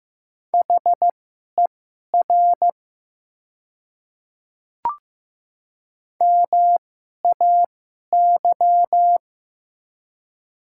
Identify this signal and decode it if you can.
Morse code, heard as a keyed tone, about 15 wpm, spaced as for 11 wpm, 700 Hz. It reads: HER MAY
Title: 2x Character Spacing - Top 100 Words - 15wpm